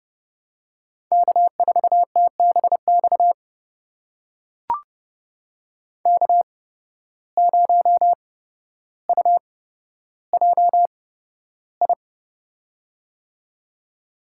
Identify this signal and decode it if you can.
Morse code, heard as a keyed tone, about 31 wpm, spaced as for 13 wpm, 700 Hz. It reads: K4TBX K0UJI
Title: USA Call Signs - 5 Characters - 8x Character Spacing - 30wpm